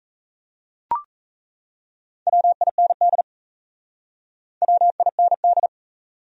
Morse code, heard as a keyed tone, about 37 wpm, 700 Hz. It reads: WIND WIND